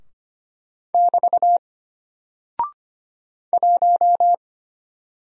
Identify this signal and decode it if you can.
Morse code, heard as a keyed tone, about 25 wpm, 700 Hz. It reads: <BT> 1
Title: Single Character (Letter:Number:Punctuation:Prosign) Alt 25wpm